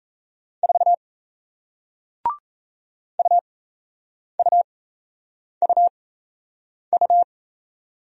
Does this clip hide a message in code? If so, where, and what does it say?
Yes: Morse code from 0.6 to 7.2 s, heard as a keyed tone, about 37 wpm, spaced as for 8 wpm, 700 Hz. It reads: 4 UUUU